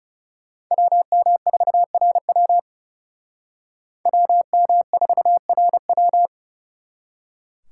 Morse code, heard as a keyed tone, about 38 wpm, 700 Hz. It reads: WM4RW WM4RW